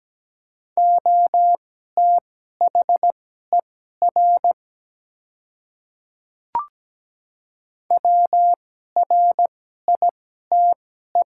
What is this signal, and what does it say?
Morse code, heard as a keyed tone, about 17 wpm, spaced as for 12 wpm, 700 Hz. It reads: OTHER WRITE
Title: 2x Character Spacing - Top 100 Words - 17wpm